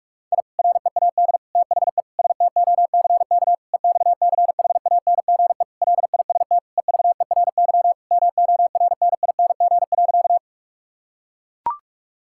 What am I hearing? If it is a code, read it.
Morse code, heard as a keyed tone, about 41 wpm, 700 Hz. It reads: I READ THE STOCK EXCHANGE LIST EVERY MORNING.
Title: Sentences from Top 1000 Words - Review Part 6 - 45wpm